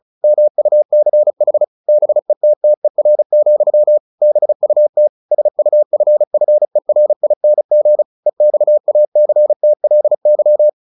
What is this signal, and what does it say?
Morse code, heard as a keyed tone, about 34 wpm, 600 Hz. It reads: MUCH BETTER, BUT SUFFERING EXACTLY
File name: the old man and the sea / the old man and the sea 1 35wpm